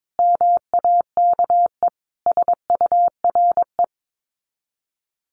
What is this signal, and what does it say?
Morse code, heard as a keyed tone, about 22 wpm, 700 Hz. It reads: MAKE SURE
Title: Sets of 5 Words Encore2 22wpm